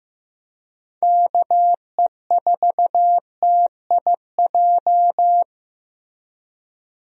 Morse code, heard as a keyed tone, about 16 wpm, 700 Hz. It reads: KE4TIJ
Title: USA Call Signs - 6 Characters - Repeated 2x - 15wpm